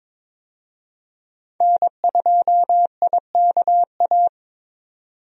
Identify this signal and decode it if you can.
Morse code, heard as a keyed tone, about 22 wpm, 700 Hz. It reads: N2IKA
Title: USA Call Signs - 5 Characters - 22wpm